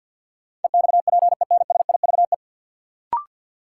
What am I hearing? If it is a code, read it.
Morse code, heard as a keyed tone, about 52 wpm, 700 Hz. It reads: EXPENSIVE
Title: Sentences from Top 1000 Words - Review Part 6 - New Word Drill - 50wpm